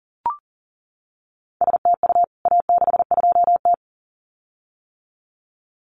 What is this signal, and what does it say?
Morse code, heard as a keyed tone, about 40 wpm, 700 Hz. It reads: STU A62T
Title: CWT Exchange 40wpm